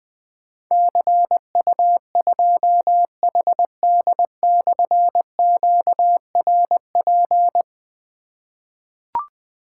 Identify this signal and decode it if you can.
Morse code, heard as a keyed tone, about 20 wpm, 700 Hz. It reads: CU2HD/QRP
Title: Difficult Call Signs 20wpm